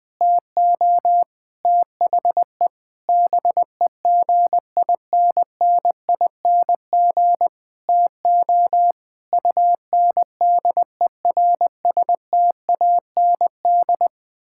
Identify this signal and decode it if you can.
Morse code, heard as a keyed tone, about 20 wpm, 700 Hz. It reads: TO THE BEGINNING TO UNDERSTAND
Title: Sentences with 8-Word Phrases - 20wpm